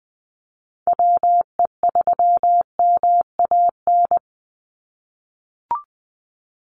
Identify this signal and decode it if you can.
Morse code, heard as a keyed tone, about 20 wpm, 700 Hz. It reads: WE3MAN
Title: USA Call Signs 20wpm